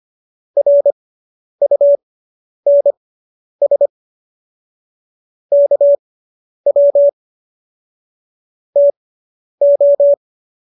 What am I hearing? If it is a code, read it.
Morse code, heard as a keyed tone, about 25 wpm, spaced as for 10 wpm, 570 Hz. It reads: RUNS KW TO